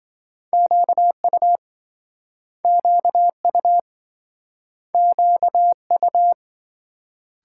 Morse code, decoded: QU QU QU

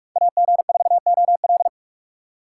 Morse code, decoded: AG4OL

